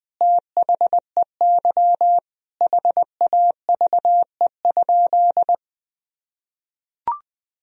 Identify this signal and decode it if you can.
Morse code, heard as a keyed tone, about 20 wpm, 700 Hz. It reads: THEY HAVE?
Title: Sentences from Top 1000 Words - Review Part 3 - 20wpm